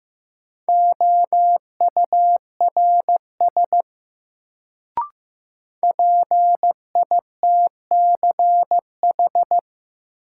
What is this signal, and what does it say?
Morse code, heard as a keyed tone, about 15 wpm, 700 Hz. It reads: OURS PITCH
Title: Sentences from Top 1300 Words - New Word Drill Alt - 15wpm